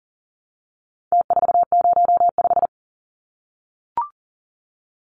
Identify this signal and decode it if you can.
Morse code, heard as a keyed tone, about 40 wpm, 700 Hz. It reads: T405
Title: Numbers - Contest Style - 4-Digit 40wpm